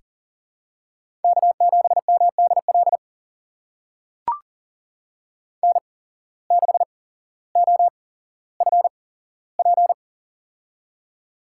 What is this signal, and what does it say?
Morse code, heard as a keyed tone, about 40 wpm, spaced as for 17 wpm, 700 Hz. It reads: K7MDL N6OFP
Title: USA Call Signs - 8x Character Spacing - 40wpm